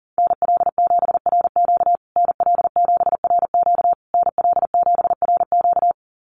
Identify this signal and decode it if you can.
Morse code, heard as a keyed tone, about 40 wpm, 700 Hz. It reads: NL7RQ NL7RQ NL7RQ
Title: USA Call Signs Repeated 3x 40wpm